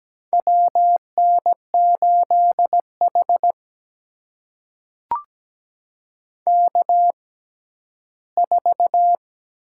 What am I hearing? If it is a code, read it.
Morse code, heard as a keyed tone, about 17 wpm, spaced as for 11 wpm, 700 Hz. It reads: WN8H K4
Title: USA Call Signs - 4 Characters - 6x Character Spacing - 17wpm